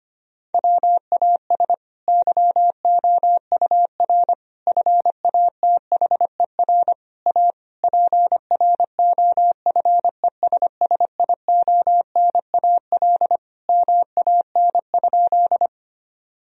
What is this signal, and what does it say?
Morse code, heard as a keyed tone, about 25 wpm, 700 Hz. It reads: WAS YOUR FATHER A PROFESSIONAL MAN?
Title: Sentences from Top 1000 Words - Review Part 4 - 25wpm